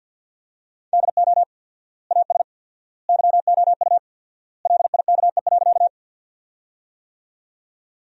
Morse code, heard as a keyed tone, about 47 wpm, 700 Hz. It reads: DO AS YOU LIKE.
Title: 4x Word-Spacing - Sentences from Top 100 Words - 50wpm